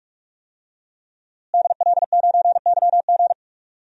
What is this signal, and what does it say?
Morse code, heard as a keyed tone, about 46 wpm, 700 Hz. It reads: DL9YG
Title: DXCC-100 Call Signs - 5 Characters - Repeated 2x - 45wpm